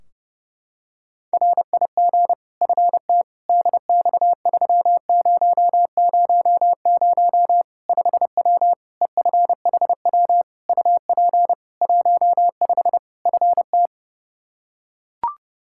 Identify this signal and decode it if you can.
Morse code, heard as a keyed tone, about 30 wpm, 700 Hz. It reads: RIG FT DX3000 5W EFHW UP 15 FT
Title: K1USN SST QSO Practice 30wpm